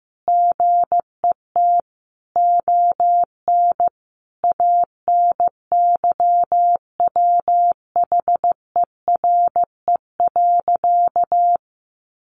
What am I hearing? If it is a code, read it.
Morse code, heard as a keyed tone, about 15 wpm, 700 Hz. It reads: GET ON ANYWHERE.
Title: Sentences from Top 1400 Words 15wpm